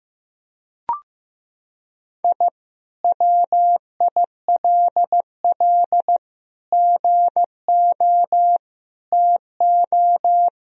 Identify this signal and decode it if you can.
Morse code, heard as a keyed tone, about 15 wpm, 700 Hz. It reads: I WILL GO TO
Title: Sentences from Top 1000 Words - Review Part 2 - 15wpm